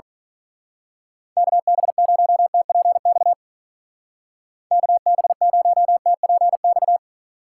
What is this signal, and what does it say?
Morse code, heard as a keyed tone, about 39 wpm, 700 Hz. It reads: KB0TPX KB0TPX
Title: Speed Racing - USA 2x3 Call Signs 35wpm